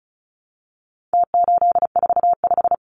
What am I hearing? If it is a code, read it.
Morse code, heard as a keyed tone, about 34 wpm, 700 Hz. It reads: T845
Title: Numbers - Contest Style - 4-Digit 35wpm